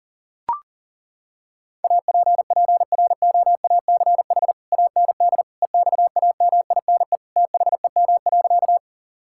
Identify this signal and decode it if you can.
Morse code, heard as a keyed tone, about 37 wpm, 700 Hz. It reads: APPROACH AND EXAMINE THEM.
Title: Sentences from Top 1600 Words 40wpm